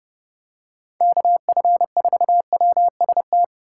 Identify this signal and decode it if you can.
Morse code, heard as a keyed tone, about 31 wpm, 700 Hz. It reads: KF4WST
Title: USA Call Signs - 6 Characters - 8x Character Spacing - 30wpm